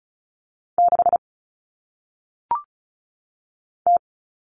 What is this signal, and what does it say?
Morse code, heard as a keyed tone, about 36 wpm, 700 Hz. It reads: 6 T
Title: All Cut Numbers - 1-Digit 35wpm